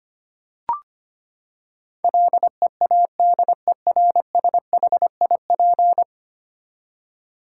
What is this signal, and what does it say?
Morse code, heard as a keyed tone, about 25 wpm, 700 Hz. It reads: LEADERSHIP
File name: Ten Letter Words - 25wpm